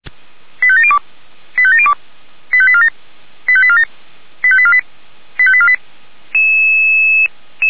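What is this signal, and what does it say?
Five-tone selective call (ZVEI), heard as quick runs of five tones, tones of about 70 ms. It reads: 85702 85702 85847 85847 85848 85848
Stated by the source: ZVEI five-tone selective call received on SDR.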